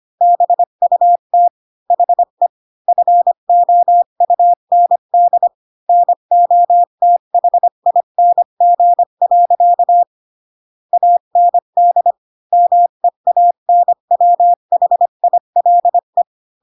Morse code, heard as a keyed tone, about 25 wpm, 700 Hz. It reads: BUT HE FOUND NOTHING. AND MEANWHILE
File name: Pinocchio / Pinocchio ch5 20 25 ik1hge